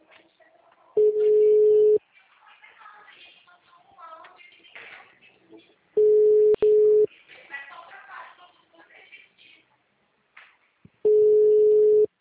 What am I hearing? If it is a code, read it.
Ringback tone (Europe, 425 Hz), with speech.
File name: out- -2031- - - .68685